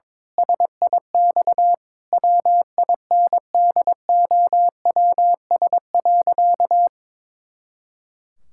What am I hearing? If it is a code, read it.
Morse code, heard as a keyed tone, about 22 wpm, 700 Hz. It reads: SIX WINDOWS.
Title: Pangram Challenge v2 - 22wpm